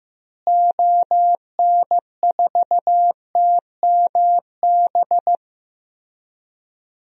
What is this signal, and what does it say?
Morse code, heard as a keyed tone, about 15 wpm, 700 Hz. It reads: ON4TMB
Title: DXCC-100 Call Signs - 6 Characters - Repeated 2x - 15wpm